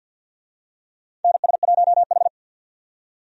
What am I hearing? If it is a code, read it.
Morse code, heard as a keyed tone, about 51 wpm, 700 Hz. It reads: NS1H
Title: USA Call Signs - 4 Characters - 50wpm